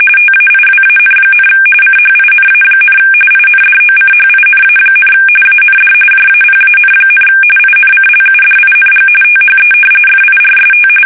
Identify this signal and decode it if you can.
RTTY, about 45.45 baud. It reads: TAIWAN, CHINA, INDOCHINA, AND JAPAN, INCLUDING THE FIRST
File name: fsk850 / battleshipnj